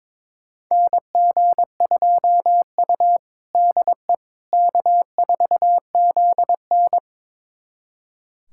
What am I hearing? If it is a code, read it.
Morse code, heard as a keyed tone, about 22 wpm, 700 Hz. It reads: NG2U DE K4ZN
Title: USA Callsign DE Callsign - 4 Characters - 22wpm